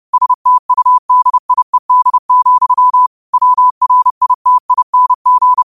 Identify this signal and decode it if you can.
Morse code, heard as a keyed tone, about 30 wpm, 1000 Hz. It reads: STUDIED, WRITING
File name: 30wpm